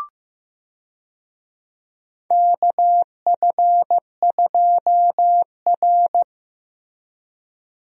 Morse code, heard as a keyed tone, about 15 wpm, 700 Hz. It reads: KF2R
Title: USA Call Signs - 4 Characters - Repeated 2x - 15wpm